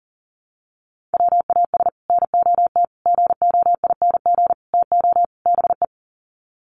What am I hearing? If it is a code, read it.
Morse code, heard as a keyed tone, about 37 wpm, 700 Hz. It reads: WAS NOT GOING TO BE
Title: Sets of 5 Words Encore 40wpm